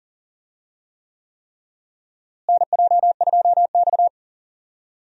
Morse code, heard as a keyed tone, about 40 wpm, 700 Hz. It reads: NJ2X
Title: USA Call Signs - 4 Characters - Repeated 3x - 40wpm